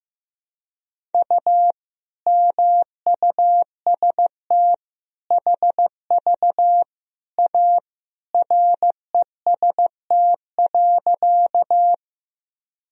Morse code, heard as a keyed tone, about 15 wpm, 700 Hz. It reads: U MUST HV A REST.
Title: Sentences from Top 500 Words and Abbreviations 15wpm